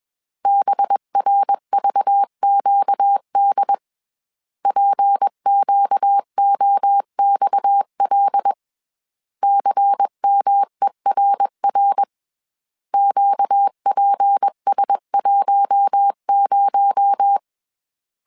Morse code, heard as a keyed tone, about 21 wpm, 800 Hz. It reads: BRVQD PQOXL CMERR QPS10
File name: CW / DET / 1 6 1500C S A 20wpm 2048